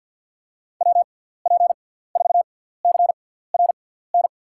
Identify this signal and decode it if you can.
Morse code, heard as a keyed tone, about 50 wpm, spaced as for 17 wpm, 700 Hz. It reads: WP4CRN
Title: USA Call Signs - 6 Characters - 6x Character Spacing - 50wpm